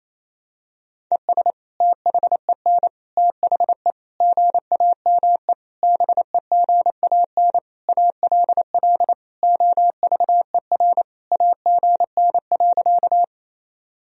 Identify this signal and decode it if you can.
Morse code, heard as a keyed tone, about 28 wpm, 700 Hz. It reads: ES THEN THE GAME BEGAN ALL OVER AGN.
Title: Sentences from Top 500 Words and Abbreviations 28wpm